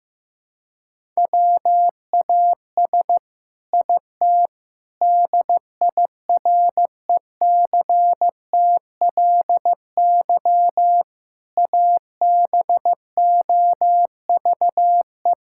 Morse code, heard as a keyed tone, about 15 wpm, 700 Hz. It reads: WAS IT DIRECTLY ABOVE
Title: Sentences from Top 1000 Words - Review Part 5 - 15wpm